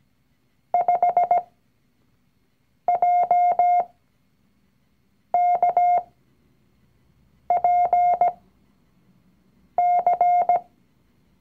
Morse code, heard as a keyed tone, about 17 wpm, spaced as for 5 wpm, 710 Hz. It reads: 5JKPC
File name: CW-Online-Kurs / Anfänger / Lektion 019